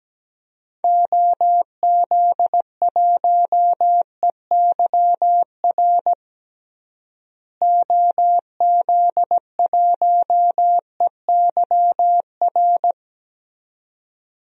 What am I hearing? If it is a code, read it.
Morse code, heard as a keyed tone, about 17 wpm, 700 Hz. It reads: OZ1EYR OZ1EYR
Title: DXCC-100 Call Signs - 6 Characters - Repeated 2x - 17wpm